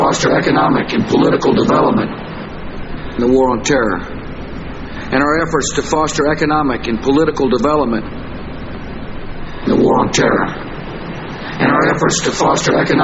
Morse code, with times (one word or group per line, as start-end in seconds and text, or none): none